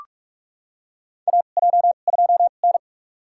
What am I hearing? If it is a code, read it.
Morse code, heard as a keyed tone, about 42 wpm, spaced as for 31 wpm, 700 Hz. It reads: AJ2N
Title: USA Call Signs - 2x Character Spacing - 45wpm